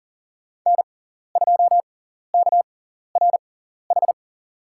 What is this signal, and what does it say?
Morse code, heard as a keyed tone, about 40 wpm, spaced as for 14 wpm, 700 Hz. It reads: N2KRH
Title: USA Call Signs - 5 Characters - 6x Character Spacing - 40wpm